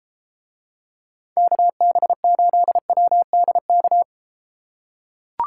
Morse code, heard as a keyed tone, about 32 wpm, 700 Hz. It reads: KB8WDK